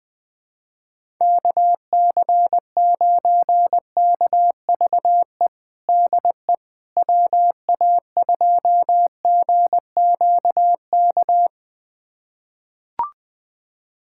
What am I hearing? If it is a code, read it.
Morse code, heard as a keyed tone, about 21 wpm, 700 Hz. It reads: KC9KVE DE WA2GQK
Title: USA Callsign DE Callsign - 6 Characters - 20wpm